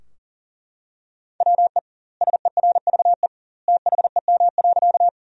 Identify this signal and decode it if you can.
Morse code, heard as a keyed tone, about 40 wpm, 700 Hz. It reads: WE SERVE THEM.
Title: 2x Word-Spacing - Sentences from Top 400 Words - 40wpm